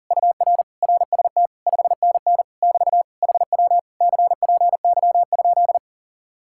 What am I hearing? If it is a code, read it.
Morse code, heard as a keyed tone, about 52 wpm, 700 Hz. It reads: UR RST 5NN <BT> HW CPY?